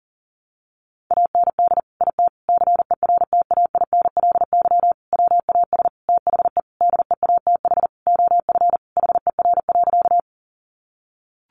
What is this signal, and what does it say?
Morse code, heard as a keyed tone, about 40 wpm, 700 Hz. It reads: AND IT CERTAINLY WAS THE DEATH OF HER.
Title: Sentences from Top 1100 Words 40wpm